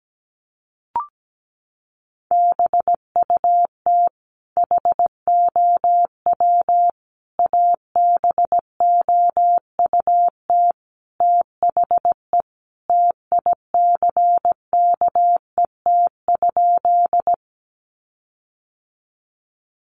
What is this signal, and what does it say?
Morse code, heard as a keyed tone, about 17 wpm, 700 Hz. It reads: BUT HOW ABOUT THE TICKET?
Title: Sentences from Top 1400 Words 17wpm